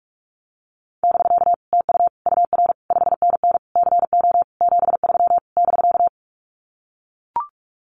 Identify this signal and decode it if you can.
Morse code, heard as a keyed tone, about 44 wpm, 700 Hz. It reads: <BK> TU UR 5NN CO 73 <BK>
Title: Common QSO Phrases Encore 45wpm